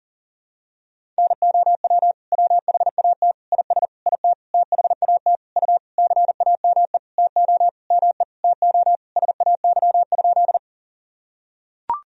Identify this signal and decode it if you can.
Morse code, heard as a keyed tone, about 40 wpm, 700 Hz. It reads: NOW WHAT IS IT THAT U CAME TO ME TO SAY?
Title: Sentences from Top 200 Words and Abbreviations 40wpm